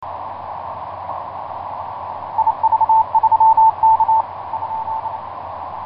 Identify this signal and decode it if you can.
Morse code, heard as a keyed tone, about 28 wpm, 890 Hz. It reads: IV3K ?
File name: vk9x-g6ay iv3kvc etc 12m cw 23feb2010 0803